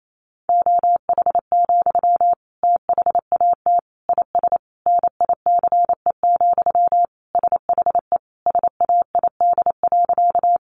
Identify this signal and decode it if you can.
Morse code, heard as a keyed tone, about 28 wpm, 700 Hz. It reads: OH, THAT IS NICE, SHE SAID.
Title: Sentences from Top 1100 Words 28wpm